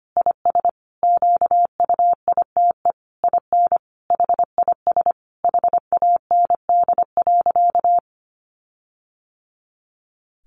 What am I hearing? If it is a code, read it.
Morse code, heard as a keyed tone, about 25 wpm, 700 Hz. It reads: IS QUITE IN HIS HAND.
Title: Sentences from Top 1400 Words 25wpm